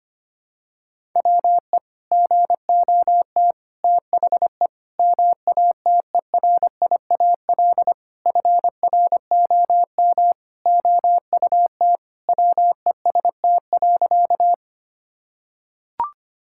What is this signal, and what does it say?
Morse code, heard as a keyed tone, about 25 wpm, 700 Hz. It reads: WE GOT THE MATERIAL FROM OUT WEST.
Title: Sentences from Top 1000 Words - Review Part 5 - 25wpm